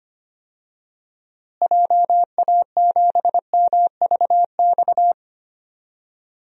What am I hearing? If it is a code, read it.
Morse code, heard as a keyed tone, about 25 wpm, 700 Hz. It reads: JA7MVX
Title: DXCC-100 Call Signs - 6 Characters - Repeated 3x - 25wpm